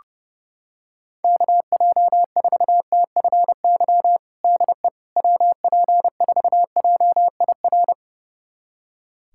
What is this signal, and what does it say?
Morse code, heard as a keyed tone, about 30 wpm, 700 Hz. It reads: KJ4TFY DE WP4JIR
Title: USA Callsign DE Callsign 30wpm